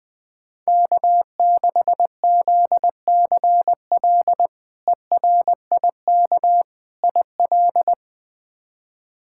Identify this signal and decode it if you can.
Morse code, heard as a keyed tone, about 21 wpm, 700 Hz. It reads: K6ZCL ERIK IL
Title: K1USN SST QSO Practice - 20wpm